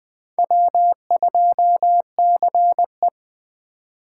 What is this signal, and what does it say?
Morse code, heard as a keyed tone, about 21 wpm, 700 Hz. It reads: W2CE
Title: USA Call Signs - 4 Characters - 4x Character Spacing - 20wpm